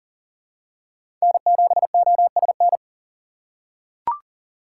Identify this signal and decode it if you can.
Morse code, heard as a keyed tone, about 40 wpm, 700 Hz. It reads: N7OSN
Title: USA Call Signs - 4x Character Spacing - 40wpm